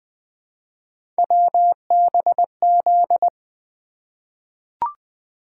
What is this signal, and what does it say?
Morse code, heard as a keyed tone, about 20 wpm, 700 Hz. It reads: WBZ
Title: Three Letter-Number Combinations - 20wpm